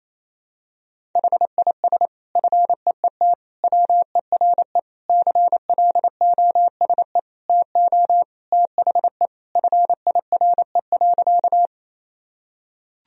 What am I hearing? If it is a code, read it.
Morse code, heard as a keyed tone, about 28 wpm, 700 Hz. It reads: HIS FEET WERE CLOSE TO THE FIRE.
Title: Sentences from Top 1000 Words - Review Part 1 - 28wpm